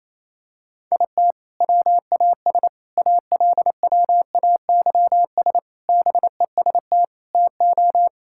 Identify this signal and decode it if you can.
Morse code, heard as a keyed tone, about 28 wpm, 700 Hz. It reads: IT WAS ALWAYS BEST TO
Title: Sentences from Top 1000 Words - Review Part 6 - 28wpm